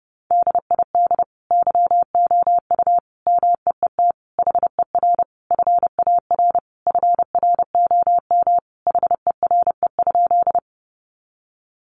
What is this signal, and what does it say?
Morse code, heard as a keyed tone, about 30 wpm, 700 Hz. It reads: DID YOU MEET HER FAR FROM HERE?
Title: Sentences from Top 1100 Words 30wpm